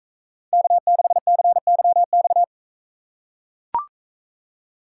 Morse code, heard as a keyed tone, about 40 wpm, 700 Hz. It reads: K6CYX